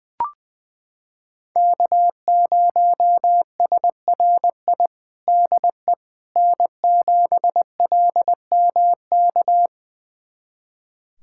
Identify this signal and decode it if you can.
Morse code, heard as a keyed tone, about 21 wpm, 700 Hz. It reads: K0SRI DE N7LMK
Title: USA Callsign DE Callsign - 5 Characters - 20wpm